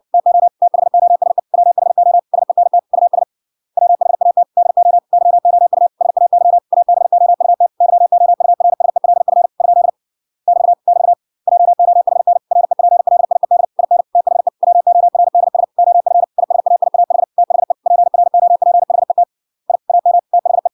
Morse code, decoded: TO THOSE WHO SENT US WHAT DO YOU SAY ABOUT YOURSELF ? <BT> <BT> JOHN REPLIED IN THE WORDS OF ISAIAH THE PROPHET I AM THE

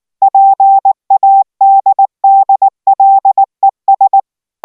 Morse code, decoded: PADDLES

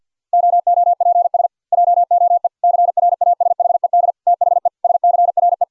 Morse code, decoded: OOPS JOE CRASHED THE SCRE

